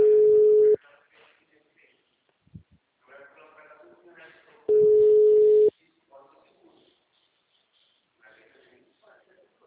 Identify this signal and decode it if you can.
Ringback tone (Europe, 425 Hz), with speech.
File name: out- -2016- - - .56681